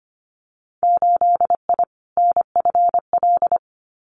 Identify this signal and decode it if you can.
Morse code, heard as a keyed tone, about 25 wpm, 700 Hz. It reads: 8I NFL
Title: Winter Field Day Exchanges Encore 25wpm